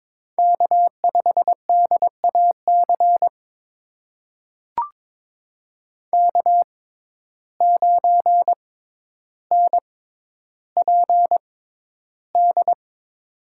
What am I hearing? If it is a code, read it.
Morse code, heard as a keyed tone, about 22 wpm, spaced as for 11 wpm, 700 Hz. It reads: K5DAC K9NPD
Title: USA Call Signs - 5 Characters - 6x Character Spacing - 22wpm